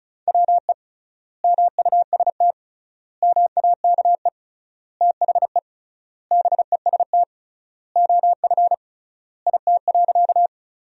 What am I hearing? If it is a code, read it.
Morse code, heard as a keyed tone, about 34 wpm, 700 Hz. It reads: WE MUST MAKE THE BEST OF IT.
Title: 3x Word-Spacing - Sentences from Top 400 Words - 35wpm